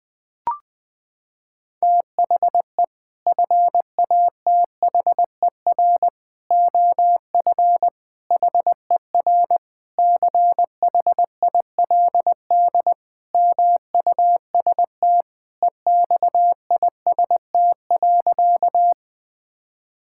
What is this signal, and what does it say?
Morse code, heard as a keyed tone, about 21 wpm, 700 Hz. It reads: THE FATHER OF HER CHILD MUST EXIST.
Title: Sentences from Top 1000 Words - Review Part 5 - 20wpm